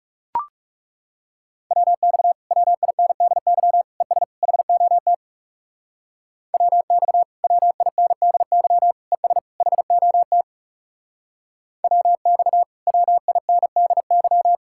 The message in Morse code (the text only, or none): WX WINDY ES HOT WX WINDY ES HOT WX WINDY